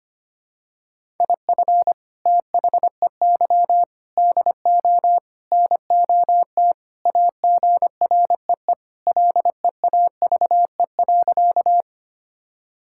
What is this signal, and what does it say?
Morse code, heard as a keyed tone, about 25 wpm, 700 Hz. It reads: IF THEY DO NOT AGREE LEAVE.
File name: practice / 25wpm / Sentences with 4-Word Phrases - 25wpm